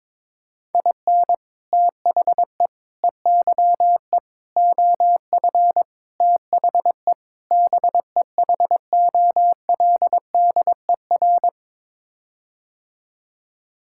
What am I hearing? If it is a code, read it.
Morse code, heard as a keyed tone, about 22 wpm, 700 Hz. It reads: IN THE EYE OF THE BEHOLDER
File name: practice / 22wpm / Sets of 6 Words - 22wpm